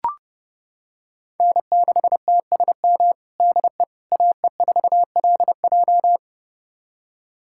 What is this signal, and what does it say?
Morse code, heard as a keyed tone, about 30 wpm, 700 Hz. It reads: N6TSM DE AE4LJ
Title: USA Callsign DE Callsign - 5 Characters - 30wpm